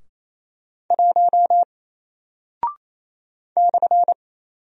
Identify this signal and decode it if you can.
Morse code, heard as a keyed tone, about 27 wpm, 700 Hz. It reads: 1 /
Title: Single Character (Letter:Number:Punctuation:Prosign) 28wpm